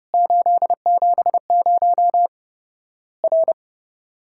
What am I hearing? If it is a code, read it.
Morse code, heard as a keyed tone, about 30 wpm, 700 Hz. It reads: 870 R
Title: Worked All Europe DX Contest - QTC Exchange - 30wpm